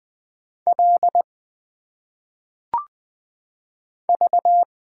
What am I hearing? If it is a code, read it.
Morse code, heard as a keyed tone, about 20 wpm, 700 Hz. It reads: L V
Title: Single Letters - Phonetic Pronunciation 20wpm